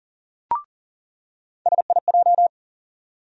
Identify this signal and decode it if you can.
Morse code, heard as a keyed tone, about 40 wpm, 700 Hz. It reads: SIJ
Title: USA Call Sign Postfixes - 3 Letters 40wpm